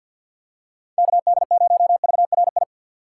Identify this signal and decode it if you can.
Morse code, heard as a keyed tone, about 46 wpm, 700 Hz. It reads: KD0VRI
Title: USA Call Signs - 6 Characters - 6x Character Spacing - 50wpm